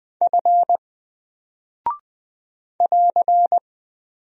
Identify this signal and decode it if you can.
Morse code, heard as a keyed tone, about 20 wpm, 700 Hz. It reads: F <AR>
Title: Single Character (Letter:Number:Punctuation:Prosign) 20wpm